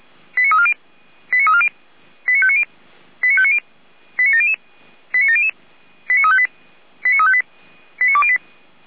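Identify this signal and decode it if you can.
Five-tone selective call (ZVEI), heard as quick runs of five tones, tones of about 70 ms. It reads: 89340 89340 89590 89590 89700 89700 89358 89358 89208
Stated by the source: Five-tone ZVEI selective call received on SDR by Feuerwehr Orlamünde.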